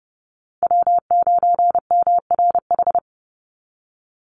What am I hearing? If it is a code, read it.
Morse code, heard as a keyed tone, about 30 wpm, 700 Hz. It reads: W9MRH
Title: NA SOTA Chaser Callsigns 30wpm